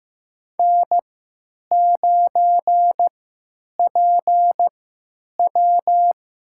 Morse code, heard as a keyed tone, about 15 wpm, spaced as for 9 wpm, 700 Hz. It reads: N9PW
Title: USA Call Signs - 3x Character Spacing - 15wpm